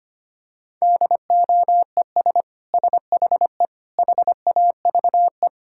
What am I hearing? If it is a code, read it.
Morse code, heard as a keyed tone, about 25 wpm, 700 Hz. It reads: DOES SHE HAVE
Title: Sentences from Top 1600 Words 25wpm